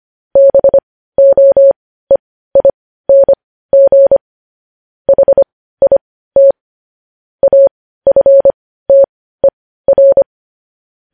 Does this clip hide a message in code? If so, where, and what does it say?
Yes: Morse code from 0.4 to 10.2 s, heard as a keyed tone, about 25 wpm, spaced as for 15 wpm, 550 Hz. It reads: BOEING HIT AFTER